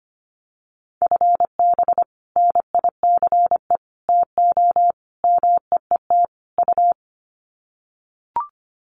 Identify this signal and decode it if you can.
Morse code, heard as a keyed tone, about 25 wpm, 700 Hz. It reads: FB NICE TO MEET U
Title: Common QSO Phrases 25wpm